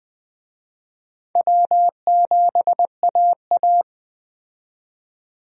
Morse code, heard as a keyed tone, about 21 wpm, 700 Hz. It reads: W7AA
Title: USA Call Signs - 4 Characters - 20wpm